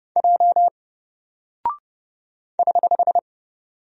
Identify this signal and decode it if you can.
Morse code, heard as a keyed tone, about 30 wpm, 700 Hz. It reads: J <HH>
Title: Single Character (Letter:Number:Punctuation:Prosign) 30wpm